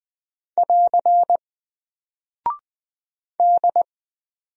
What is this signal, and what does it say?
Morse code, heard as a keyed tone, about 20 wpm, 700 Hz. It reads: <AR> D
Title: Single Character (Letter:Number:Punctuation:Prosign) Alt 20wpm